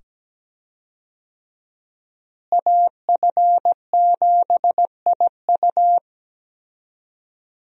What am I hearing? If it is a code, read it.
Morse code, heard as a keyed tone, about 17 wpm, 700 Hz. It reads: AF7IU
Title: USA Call Signs - 5 Characters - Repeated 2x - 17wpm